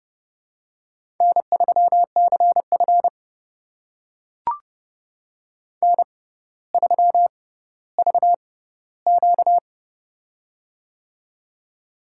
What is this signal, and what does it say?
Morse code, heard as a keyed tone, about 30 wpm, spaced as for 15 wpm, 700 Hz. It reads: N3CF N3VQ
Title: USA Call Signs - 4 Characters - 6x Character Spacing - 30wpm